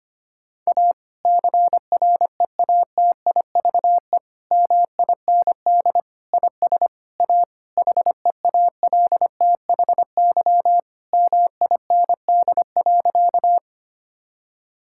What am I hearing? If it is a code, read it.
Morse code, heard as a keyed tone, about 25 wpm, 700 Hz. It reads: A CREATIVE MIND IS A HEALTHY MIND.
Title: Sentences from Top 1000 Words - Review Part 5 - 25wpm